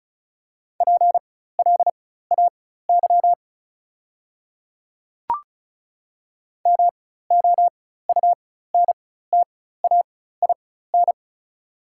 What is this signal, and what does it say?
Morse code, heard as a keyed tone, about 34 wpm, spaced as for 16 wpm, 700 Hz. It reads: PLAY MOUNTAIN
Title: 4x Character Spacing - Top 300 Words - 35wpm